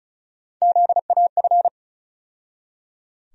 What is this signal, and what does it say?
Morse code, heard as a keyed tone, about 36 wpm, 700 Hz. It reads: ZAF
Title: USA Call Sign Postfixes - 3 Letters 35wpm